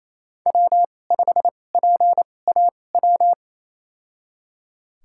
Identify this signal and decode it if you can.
Morse code, heard as a keyed tone, about 28 wpm, spaced as for 20 wpm, 700 Hz. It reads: W5PAW
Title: USA Call Signs - 2x Character Spacing - 28wpm